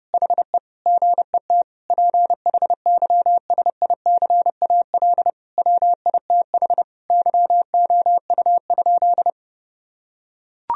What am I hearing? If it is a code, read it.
Morse code, heard as a keyed tone, about 30 wpm, 700 Hz. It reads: HE GET PHYSICAL WITH YOU?
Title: Sentences from Top 1000 Words - Review Part 4 - 30wpm